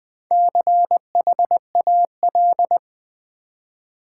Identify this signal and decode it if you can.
Morse code, heard as a keyed tone, about 20 wpm, 700 Hz. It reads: CHAL
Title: Build A Word - 4 Letters At A Time - 11-Letter Words - 20wpm